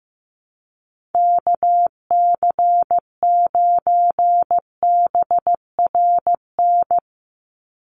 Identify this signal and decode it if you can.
Morse code, heard as a keyed tone, about 15 wpm, 700 Hz. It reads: KC9BRN
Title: USA Call Signs 15wpm